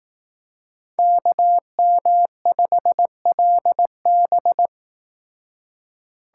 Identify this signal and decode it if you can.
Morse code, heard as a keyed tone, about 18 wpm, 700 Hz. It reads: KM5LB
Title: Speed Racing - USA 2x2 Call Signs 15wpm